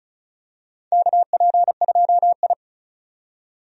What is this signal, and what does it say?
Morse code, heard as a keyed tone, about 35 wpm, 700 Hz. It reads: KP2I
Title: DXCC-100 Call Signs - 4 Characters - 35wpm